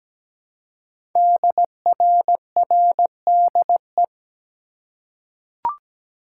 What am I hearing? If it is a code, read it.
Morse code, heard as a keyed tone, about 17 wpm, 700 Hz. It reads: DRRDE
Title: Code Groups - 5 Letters - 17wpm